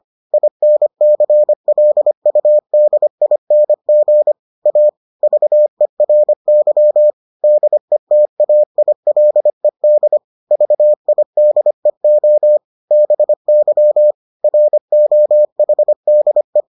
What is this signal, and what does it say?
Morse code, heard as a keyed tone, about 25 wpm, 600 Hz. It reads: INCLUDING A VERY DETAILED VIDEO BY ROHDE